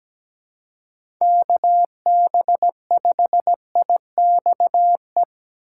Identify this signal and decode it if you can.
Morse code, heard as a keyed tone, about 17 wpm, 700 Hz. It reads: KB5IXE
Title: USA Call Signs - 4x Character Spacing - 17wpm